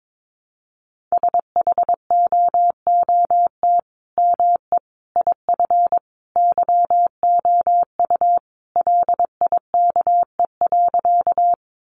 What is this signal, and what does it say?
Morse code, heard as a keyed tone, about 22 wpm, 700 Hz. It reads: SHOOT ME IF YOU LIKE.
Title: Sentences from Top 1200 Words 22wpm